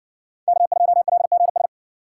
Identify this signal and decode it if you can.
Morse code, heard as a keyed tone, about 60 wpm, 700 Hz. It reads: K2LGS